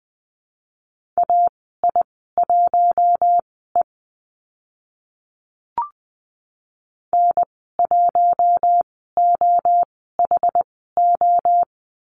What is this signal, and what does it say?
Morse code, heard as a keyed tone, about 20 wpm, spaced as for 14 wpm, 700 Hz. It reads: AI1E N1OHO
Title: 062 Lesson 11g - Letter h - Call Signs 20wpm